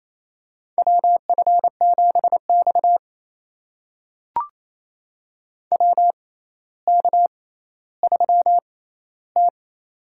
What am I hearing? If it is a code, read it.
Morse code, heard as a keyed tone, about 28 wpm, spaced as for 14 wpm, 700 Hz. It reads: WF7X WK3T
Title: USA Call Signs - 4 Characters - 6x Character Spacing - 28wpm